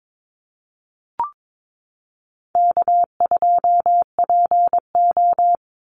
Morse code, heard as a keyed tone, about 22 wpm, 700 Hz. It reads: K2PO
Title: Call Signs Repeated 3x - 22wpm